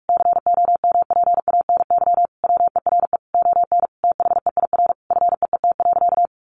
Morse code, heard as a keyed tone, about 44 wpm, 700 Hz. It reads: COMPANY WERE ON THEIR FEET.
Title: Sentences from Top 1000 Words 45wpm